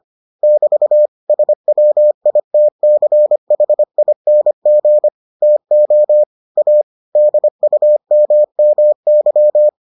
Morse code, heard as a keyed tone, about 25 wpm, 600 Hz. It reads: <BT> SWITCHING TO A DUMMY